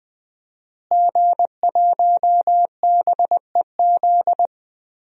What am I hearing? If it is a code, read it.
Morse code, heard as a keyed tone, about 20 wpm, 700 Hz. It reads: G1BEZ
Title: DXCC-100 Call Signs - 5 Characters - Repeated 2x - 20wpm